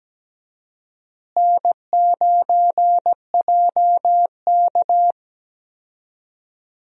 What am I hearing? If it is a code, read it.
Morse code, heard as a keyed tone, about 17 wpm, 700 Hz. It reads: N9JK
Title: USA Call Signs - 4 Characters - Repeated 2x - 17wpm